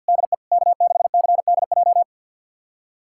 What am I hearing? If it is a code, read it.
Morse code, heard as a keyed tone, about 52 wpm, 700 Hz. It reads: DE K6CDJ
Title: USA Callsign DE Callsign 50wpm